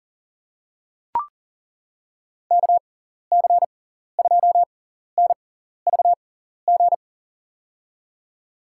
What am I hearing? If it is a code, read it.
Morse code, heard as a keyed tone, about 40 wpm, spaced as for 14 wpm, 700 Hz. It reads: KC2NVG
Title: USA Call Signs - 6 Characters - 6x Character Spacing - 40wpm